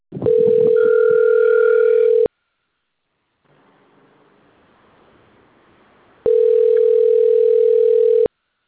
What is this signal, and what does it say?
Ringback tone (North America), 440 + 480 Hz.